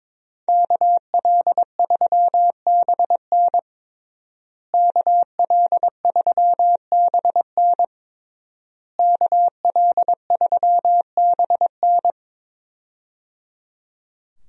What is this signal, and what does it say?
Morse code, heard as a keyed tone, about 22 wpm, 700 Hz. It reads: KL3BN KL3BN KL3BN
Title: USA Call Signs - 5 Characters - Repeated 3x - 22wpm